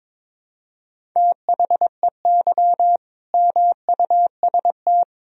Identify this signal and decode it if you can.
Morse code, heard as a keyed tone, about 22 wpm, 700 Hz. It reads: THEY MUST
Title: Sentences from Top 1000 Words - Review Part 1 - 22wpm